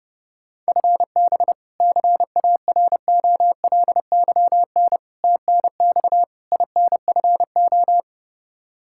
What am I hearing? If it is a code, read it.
Morse code, heard as a keyed tone, about 30 wpm, 700 Hz. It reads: FB CAROLYN TNX INFO
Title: Practice QSO - Exchange 3 - More Info - 30wpm